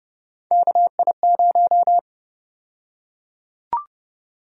Morse code, heard as a keyed tone, about 32 wpm, 700 Hz. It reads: KI0
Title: USA Call Sign Prefixes - 30wpm